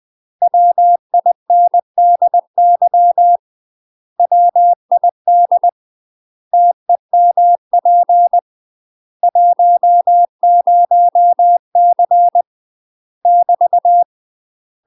Morse code, heard as a keyed tone, about 20 wpm, 700 Hz. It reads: WINDY WID TEMP 10C <BT>
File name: UK MASTER QSOs TRAINING / 20WPM / UK MASTER QSOs 0029